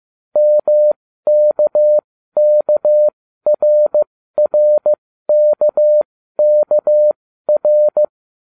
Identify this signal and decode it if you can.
Morse code, heard as a keyed tone, about 15 wpm, spaced as for 12 wpm, 600 Hz. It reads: MKKRRKKR